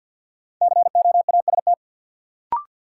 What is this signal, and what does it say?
Morse code, heard as a keyed tone, about 53 wpm, 700 Hz. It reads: COAST
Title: Sentences from Top 1300 Words - New Word Drill Alt - 50wpm